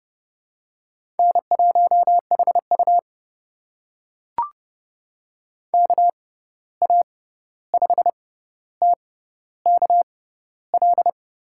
Morse code, heard as a keyed tone, about 32 wpm, spaced as for 14 wpm, 700 Hz. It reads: N1HU KA5TKL
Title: USA Call Signs - 6x Character Spacing - 30wpm